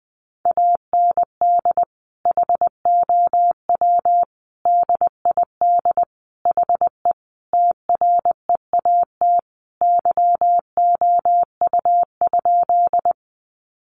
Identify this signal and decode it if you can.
Morse code, heard as a keyed tone, about 20 wpm, 700 Hz. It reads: AND HOW DID HE TREAT YOU?
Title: Sentences from Top 1300 Words 20wpm